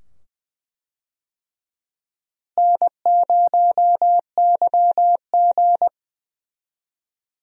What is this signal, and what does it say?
Morse code, heard as a keyed tone, about 20 wpm, 700 Hz. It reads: N0YG
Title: USA Call Signs - 4 Characters - Repeated 2x - 20wpm